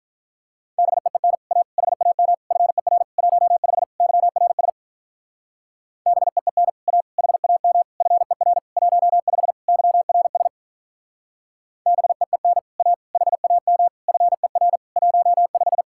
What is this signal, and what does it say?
Morse code, heard as a keyed tone, about 44 wpm, 700 Hz. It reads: BEEN A HAM FER 15 YRS BEEN A HAM FER 15 YRS BEEN A HAM FER 15